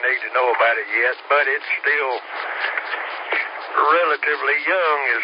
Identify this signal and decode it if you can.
Emergency Alert System (SAME) tones, heard as harsh buzzing bursts, with speech. It is heard from 0.7 to 4.7 s.